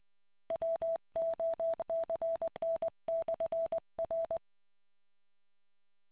Morse code, heard as a keyed tone, about 20 wpm, 660 Hz. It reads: WOCN/R